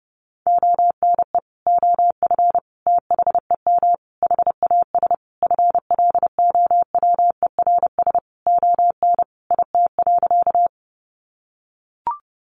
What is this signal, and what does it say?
Morse code, heard as a keyed tone, about 30 wpm, 700 Hz. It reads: ONE OF THEM HAS FLOWERS ON IT.
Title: Additional Easy Phrases 30wpm